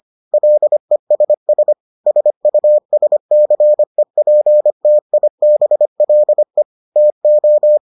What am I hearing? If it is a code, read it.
Morse code, heard as a keyed tone, about 25 wpm, 600 Hz. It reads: LESS SUSCEPTIBLE TO